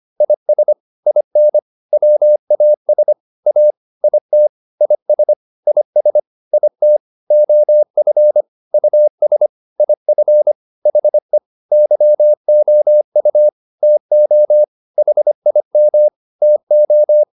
Morse code, heard as a keyed tone, about 25 wpm, 600 Hz. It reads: IS IN WAS A IT IS IS IT OF US IF HE YOU TO HIM TO